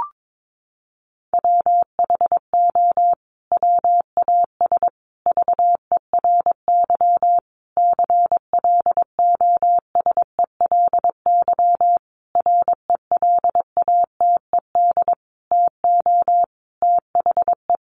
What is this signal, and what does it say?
Morse code, heard as a keyed tone, about 22 wpm, 700 Hz. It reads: WHO WAS VERY CLOSELY RELATED TO THE
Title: Sentences from Top 1400 Words 22wpm